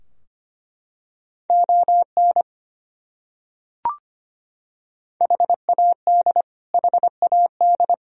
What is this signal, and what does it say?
Morse code, heard as a keyed tone, about 25 wpm, 700 Hz. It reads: ON HAD HAD
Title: Top 100 Words Repeated 25wpm